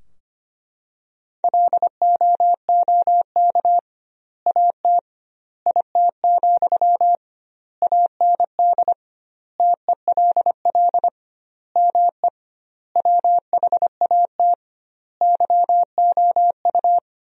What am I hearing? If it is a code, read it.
Morse code, heard as a keyed tone, about 25 wpm, 700 Hz. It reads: LOOK AT IT, AND TELL ME WHAT YOU
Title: 2x Word-Spacing - Sentences from Top 200 Words - 25wpm